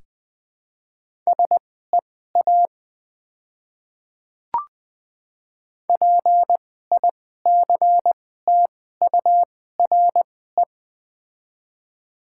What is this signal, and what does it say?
Morse code, heard as a keyed tone, about 20 wpm, spaced as for 14 wpm, 700 Hz. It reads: SEA PICTURE
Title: 2x Character Spacing - Top 300 Words - 20wpm